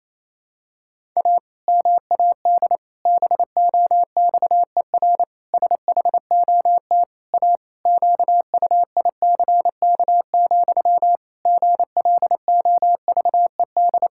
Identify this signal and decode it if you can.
Morse code, heard as a keyed tone, about 28 wpm, 700 Hz. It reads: A MAD BOXER SHOT A QUICK, GLOVED
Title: Pangram Challenge v2 - 28wpm